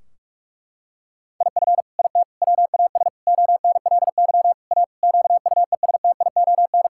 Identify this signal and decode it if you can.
Morse code, heard as a keyed tone, about 42 wpm, 700 Hz. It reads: IF IT WAS ONLY A QUESTION
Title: Sentences from Top 1600 Words 45wpm